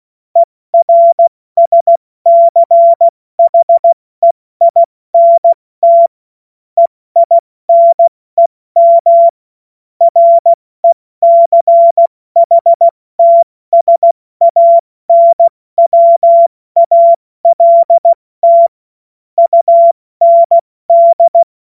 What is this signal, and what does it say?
Morse code, heard as a keyed tone, about 16 wpm, spaced as for 14 wpm, 680 Hz. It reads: ERSCHEINT EINEM RECHTSANWALT UND
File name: Morsekurs / Witze / Witz 09. August 2022 70 F80